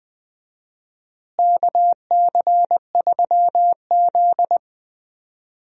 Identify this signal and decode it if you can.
Morse code, heard as a keyed tone, about 20 wpm, 690 Hz. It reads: KC3Z
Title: USA Call Signs - 4 Characters - Repeated 3x - 20wpm